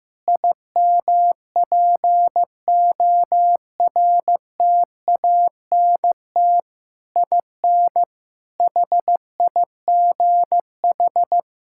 Morse code, heard as a keyed tone, about 15 wpm, 700 Hz. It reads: IMPORTANT IN HIGH
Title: Sentences from Top 1000 Words - Review Part 6 - 15wpm